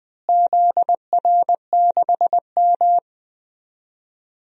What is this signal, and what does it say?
Morse code, heard as a keyed tone, about 20 wpm, 700 Hz. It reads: ZR6M
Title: DXCC-100 Call Signs - 4 Characters - Repeated 2x - 20wpm